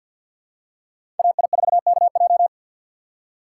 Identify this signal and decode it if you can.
Morse code, heard as a keyed tone, about 52 wpm, 700 Hz. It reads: AI4KJ
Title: USA Call Signs - 5 Characters - 4x Character Spacing - 50wpm